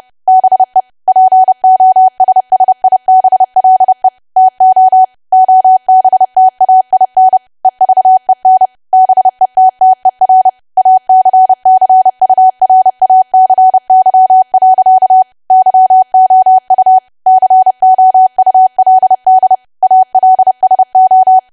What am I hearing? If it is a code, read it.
Morse code, heard as a keyed tone, about 30 wpm, 750 Hz. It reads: BE POSSIBLE TO OBTAIN EVEN BETTER ACCURACY. YOU COULD ALSO
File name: cwa / W1AW-Level3 / 30WPM